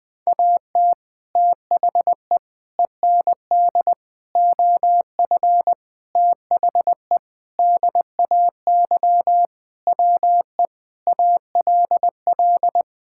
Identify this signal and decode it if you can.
Morse code, heard as a keyed tone, about 21 wpm, 700 Hz. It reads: AT THE END OF THE DAY WE ALL
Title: Sentences with 4-Word Phrases - 20wpm